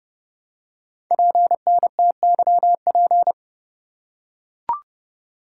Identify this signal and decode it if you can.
Morse code, heard as a keyed tone, about 30 wpm, 700 Hz. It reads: PNTYP
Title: Code Groups - 5 Letter:Numbers - 30wpm